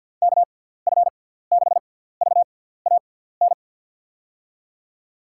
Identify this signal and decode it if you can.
Morse code, heard as a keyed tone, about 53 wpm, spaced as for 17 wpm, 700 Hz. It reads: KF6VAN
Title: USA Call Signs - 6 Characters - 6x Character Spacing - 50wpm